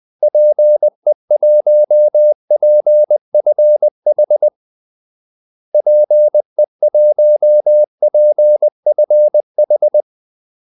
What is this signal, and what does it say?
Morse code, heard as a keyed tone, about 20 wpm, 600 Hz. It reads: PE1PFH PE1PFH
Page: Kaldesignaler 20 wpm